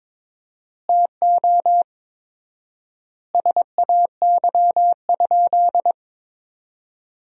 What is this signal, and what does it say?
Morse code, heard as a keyed tone, about 22 wpm, 700 Hz. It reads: TO SAY?
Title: 4x Word-Spacing - Sentences from Top 200 Words - 22wpm